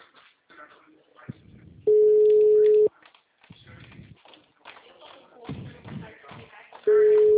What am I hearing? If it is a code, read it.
Ringback tone (Europe, 425 Hz), with speech.